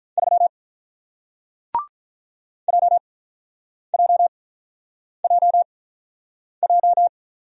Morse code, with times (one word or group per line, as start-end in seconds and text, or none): 0.2-0.5 3
2.7-7.1 JJJJ